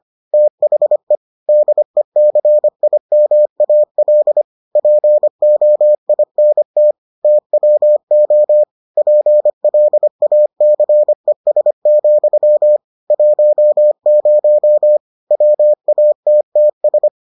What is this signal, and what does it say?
Morse code, heard as a keyed tone, about 25 wpm, 600 Hz. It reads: THE DECIMAL POINT TWO PLACES, 10 WATTS